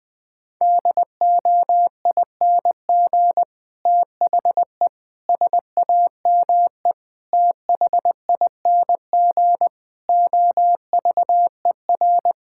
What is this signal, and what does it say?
Morse code, heard as a keyed tone, about 20 wpm, 700 Hz. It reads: DOING THE SAME THING OVER
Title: Sets of 8 Words - 20wpm